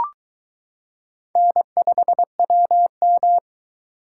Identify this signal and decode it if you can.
Morse code, heard as a keyed tone, about 22 wpm, 700 Hz. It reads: N5WM